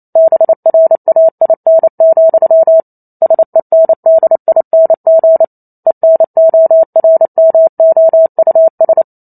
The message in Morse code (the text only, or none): BRAIN, SENDING ENORMOUS